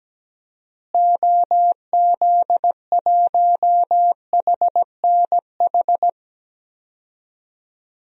Morse code, heard as a keyed tone, about 17 wpm, 700 Hz. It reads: OZ1HNH
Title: DXCC-100 Call Signs - 6 Characters - Repeated 3x - 17wpm